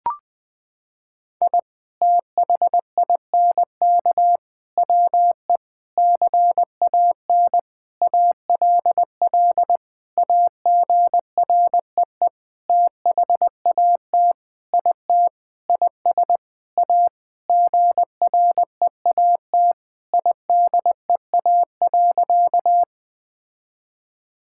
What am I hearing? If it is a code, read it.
Morse code, heard as a keyed tone, about 20 wpm, 700 Hz. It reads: I THINK WE CAN ALL AGREE THAT IT IS A GREAT IDEA.
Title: Sentences with 7-Word Phrases - 20wpm